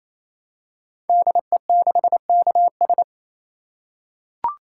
Morse code, heard as a keyed tone, about 27 wpm, 700 Hz. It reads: DE6KS
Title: Code Groups - 5 Letter:Numbers - 28wpm